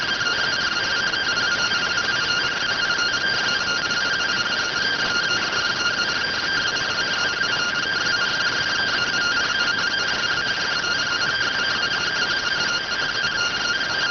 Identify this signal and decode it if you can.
Two-tone teleprinter signal, about 170 Hz shift.